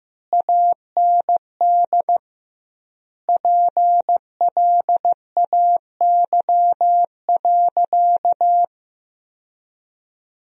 Morse code, heard as a keyed tone, about 15 wpm, 700 Hz. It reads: AND PLAY.
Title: 2x Word-Spacing - Sentences from Top 200 Words - 15wpm